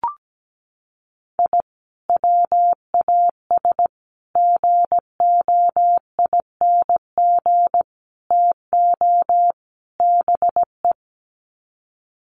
Morse code, heard as a keyed tone, about 17 wpm, 700 Hz. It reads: I WAS GOING TO BE
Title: Sets of 5 Words 17wpm